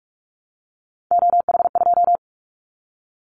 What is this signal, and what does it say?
Morse code, heard as a keyed tone, about 44 wpm, 700 Hz. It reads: OH2
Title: Callsign Prefixes for Top 50 DXCC - 45wpm